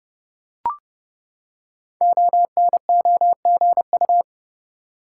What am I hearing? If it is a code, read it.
Morse code, heard as a keyed tone, about 28 wpm, 700 Hz. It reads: ONOGU
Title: Code Groups - 5 Letters - 30wpm